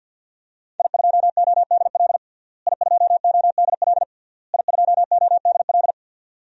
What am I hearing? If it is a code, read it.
Morse code, heard as a keyed tone, about 51 wpm, 700 Hz. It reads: I2ODL I2ODL I2ODL
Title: DXCC-100 Call Signs - 5 Characters - Repeated 3x - 50wpm